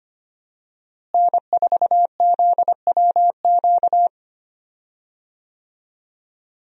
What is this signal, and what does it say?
Morse code, heard as a keyed tone, about 25 wpm, 700 Hz. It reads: N4ZWQ
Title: USA Call Signs - 5 Characters - Repeated 3x - 25wpm